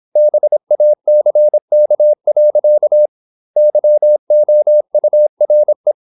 Morse code, heard as a keyed tone, about 26 wpm, 600 Hz. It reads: BACK. YOURE
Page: Never 26wpm | CW med Gnister